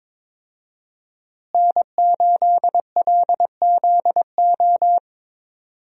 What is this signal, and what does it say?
Morse code, heard as a keyed tone, about 22 wpm, 700 Hz. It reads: N8LZO
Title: USA Call Signs - 5 Characters - Repeated 3x - 22wpm